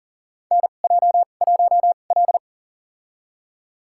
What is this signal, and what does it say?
Morse code, heard as a keyed tone, about 40 wpm, spaced as for 29 wpm, 700 Hz. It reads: NJ1L
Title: USA Call Signs - 4 Characters - 2x Character Spacing - 40wpm